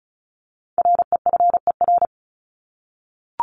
Morse code, heard as a keyed tone, about 34 wpm, 700 Hz. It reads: REFER
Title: ED Suffix - Top 100 Words with Base Words 35wpm